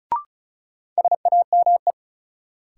Morse code, heard as a keyed tone, about 36 wpm, 700 Hz. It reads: SAME
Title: Sentences from Top 200 Words - New Word Drill - 35wpm